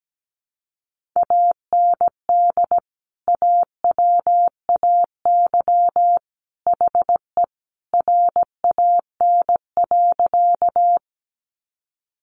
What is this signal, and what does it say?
Morse code, heard as a keyed tone, about 17 wpm, 700 Hz. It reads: AND AWAY HE RAN.
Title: Sentences from Top 600 Words 17wpm